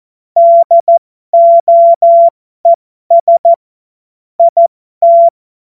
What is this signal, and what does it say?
Morse code, heard as a keyed tone, about 14 wpm, spaced as for 12 wpm, 680 Hz. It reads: DOES IT